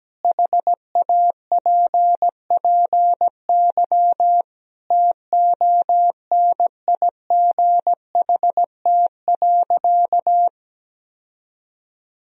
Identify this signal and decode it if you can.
Morse code, heard as a keyed tone, about 17 wpm, 700 Hz. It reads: HAPPY TONIGHT.
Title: Sentences from Top 1000 Words - Review Part 3 - 17wpm